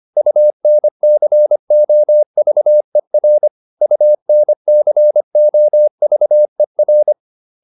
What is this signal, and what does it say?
Morse code, heard as a keyed tone, about 25 wpm, 600 Hz. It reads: UNCOVER UNCOVER